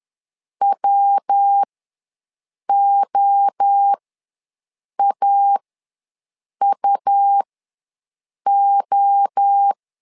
Morse code, heard as a keyed tone, about 10 wpm, spaced as for 6 wpm, 800 Hz. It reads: WOAUO
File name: LEZIONI / 3 MWRO / 3 MWRO ws 10wpm 2310